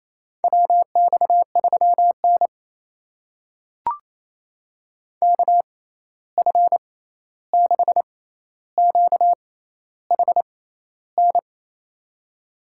Morse code, heard as a keyed tone, about 28 wpm, spaced as for 13 wpm, 700 Hz. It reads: WX3N KF6QHN
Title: USA Call Signs - 6x Character Spacing - 28wpm